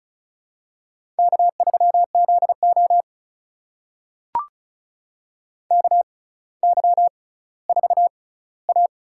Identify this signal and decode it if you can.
Morse code, heard as a keyed tone, about 36 wpm, spaced as for 18 wpm, 700 Hz. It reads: K3ZO KY4A
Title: USA Call Signs - 4 Characters - 6x Character Spacing - 35wpm